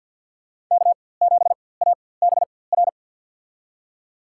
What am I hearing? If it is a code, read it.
Morse code, heard as a keyed tone, about 54 wpm, spaced as for 24 wpm, 700 Hz. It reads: K7ABR
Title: USA Call Signs - 5 Characters - 4x Character Spacing - 50wpm